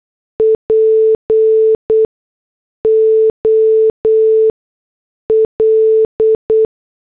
Morse code, heard as a keyed tone, about 8 wpm, 440 Hz. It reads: P O L
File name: morse / headlines